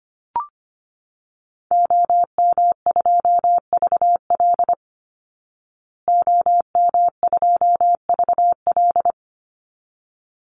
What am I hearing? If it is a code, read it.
Morse code, heard as a keyed tone, about 25 wpm, 700 Hz. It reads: OM2VL OM2VL
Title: Call Signs Repeated 3x - 25wpm